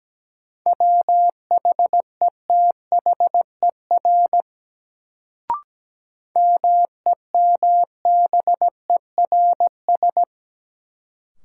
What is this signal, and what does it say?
Morse code, heard as a keyed tone, about 17 wpm, 700 Hz. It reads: WHETHER MEMBERS
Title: Sets of 3 Words - New Word Drill - 17wpm